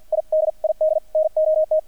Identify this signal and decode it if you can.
Morse code, heard as a keyed tone, about 50 wpm, spaced as for 44 wpm, 630 Hz. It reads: I DID NOT
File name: archived / CRSnet